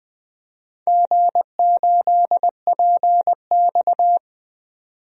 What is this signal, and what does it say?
Morse code, heard as a keyed tone, about 20 wpm, 700 Hz. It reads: G8PX
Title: DXCC-100 Call Signs - 4 Characters - 20wpm